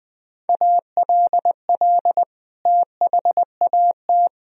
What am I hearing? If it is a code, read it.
Morse code, heard as a keyed tone, about 20 wpm, 700 Hz. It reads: ALL THAT
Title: Sentences from Top 500 Words and Abbreviations 20wpm